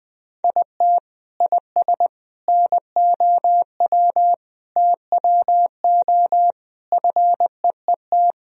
Morse code, heard as a keyed tone, about 21 wpm, 700 Hz. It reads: IT IS NOW TWO FEET
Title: Sentences from Top 1000 Words - Review Part 1 - 20wpm